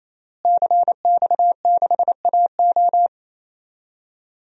Morse code, heard as a keyed tone, about 28 wpm, 700 Hz. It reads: CX6AO
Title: DXCC-100 Call Signs - 5 Characters - Repeated 3x - 28wpm